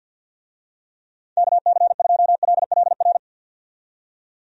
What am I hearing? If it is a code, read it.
Morse code, heard as a keyed tone, about 45 wpm, 700 Hz. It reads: KC2LLR
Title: USA Call Signs - 6 Characters - 50wpm